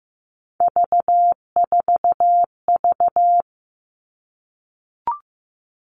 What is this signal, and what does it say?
Morse code, heard as a keyed tone, about 15 wpm, 700 Hz. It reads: V4V
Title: All Cut Numbers - 3-Digit 15wpm